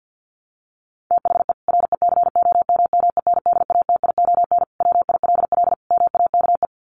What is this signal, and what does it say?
Morse code, heard as a keyed tone, about 50 wpm, 700 Hz. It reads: THE RECOMMENDATION WILL MAKE
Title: Sentences from Top 1400 Words 50wpm